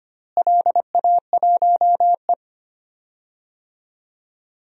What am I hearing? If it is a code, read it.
Morse code, heard as a keyed tone, about 25 wpm, 700 Hz. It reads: LA1E
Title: DXCC-100 Call Signs - 4 Characters - Repeated 3x - 25wpm